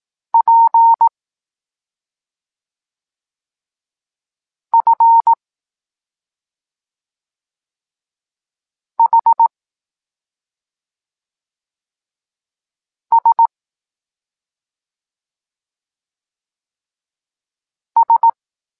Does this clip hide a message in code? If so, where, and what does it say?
Yes: Morse code from 0.3 to 18.3 s, heard as a keyed tone, about 18 wpm, 920 Hz. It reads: P F H S S